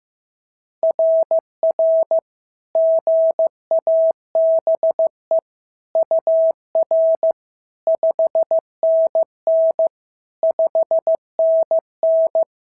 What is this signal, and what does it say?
Morse code, heard as a keyed tone, about 14 wpm, 650 Hz. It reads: RR GABE UR 5NN 5NN
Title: SOTA QSO Practice 15wpm